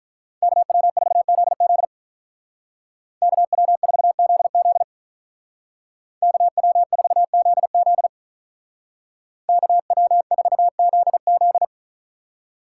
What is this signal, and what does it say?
Morse code, heard as a keyed tone, about 47 wpm, 700 Hz. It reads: KW4ZZ KW4ZZ KW4ZZ KW4ZZ